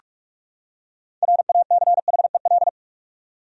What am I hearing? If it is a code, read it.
Morse code, heard as a keyed tone, about 46 wpm, 700 Hz. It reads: RACHEL
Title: USA Common Names - 45wpm